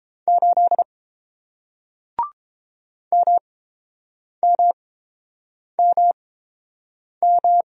Morse code, heard as a keyed tone, about 30 wpm, spaced as for 7 wpm, 700 Hz. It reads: 8 MMMM